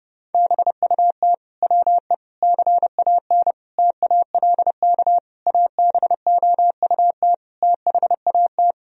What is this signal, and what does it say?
Morse code, heard as a keyed tone, about 30 wpm, 700 Hz. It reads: BUT WE CAN TALK ABOUT THAT
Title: Sentences from Top 1000 Words - Review Part 4 - 30wpm